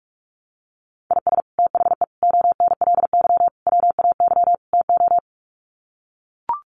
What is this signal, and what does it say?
Morse code, heard as a keyed tone, about 44 wpm, 700 Hz. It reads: IS THE ONLY WAY TO
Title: Sets of 5 Words Encore 45wpm